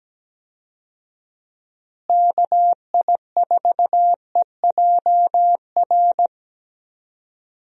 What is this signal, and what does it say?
Morse code, heard as a keyed tone, about 17 wpm, 700 Hz. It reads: KI4EJR
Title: USA Call Signs - 6 Characters - Repeated 3x - 17wpm